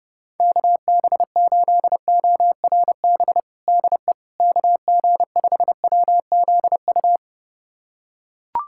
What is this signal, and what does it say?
Morse code, heard as a keyed tone, about 30 wpm, 700 Hz. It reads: KB8ORB DE KG5WZU
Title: USA Callsign DE Callsign - 6 Characters - 30wpm